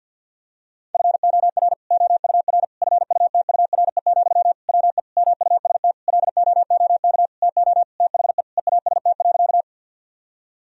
Morse code, heard as a keyed tone, about 52 wpm, 700 Hz. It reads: FOR OUR FUTURE, WE MUST LOOK TO THE EAST.
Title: Sentences from Top 1000 Words - Review Part 3 - 50wpm